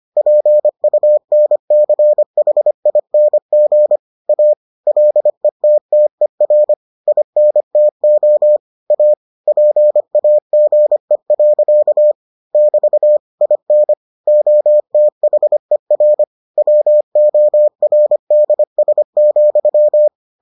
Morse code, heard as a keyed tone, about 25 wpm, 600 Hz. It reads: PUNCHING A LETTER INTO A PAGE. <BT> IN OTHER WORDS,